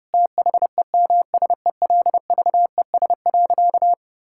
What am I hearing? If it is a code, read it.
Morse code, heard as a keyed tone, about 30 wpm, 700 Hz. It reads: THEMSELVES.
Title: Sentences from Top 1000 Words - Review Part 5 - 30wpm